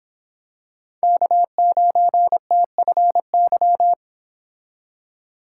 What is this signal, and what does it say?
Morse code, heard as a keyed tone, about 26 wpm, 700 Hz. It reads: K9TFY
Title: Speed Racing - USA 1x3 Call Signs 22wpm